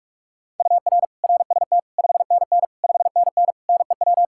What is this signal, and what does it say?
Morse code, heard as a keyed tone, about 41 wpm, 700 Hz. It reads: UR RST 5NN 5NN NEW
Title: Practice QSO - Exchange 1 - Beginning - 45wpm